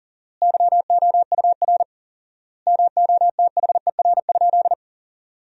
Morse code, heard as a keyed tone, about 40 wpm, 700 Hz. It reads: YOUR MOTHER?
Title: 4x Word-Spacing - Sentences from Top 200 Words - 40wpm